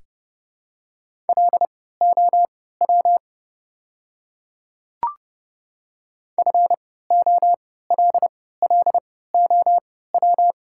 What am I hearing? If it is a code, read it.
Morse code, heard as a keyed tone, about 30 wpm, spaced as for 17 wpm, 690 Hz. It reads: LOW FOLLOW
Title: 3x Character Spacing - Top 500 Words - 30wpm